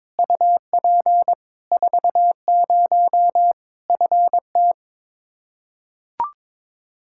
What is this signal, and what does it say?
Morse code, heard as a keyed tone, about 22 wpm, 700 Hz. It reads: UP 40 FT
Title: Practice QSO - Exchange 2 - Info - 22wpm